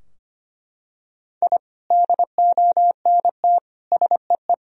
Morse code, heard as a keyed tone, about 25 wpm, 700 Hz. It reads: I DONT SEE
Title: Sentences from Top 1000 Words - Review Part 2 - 25wpm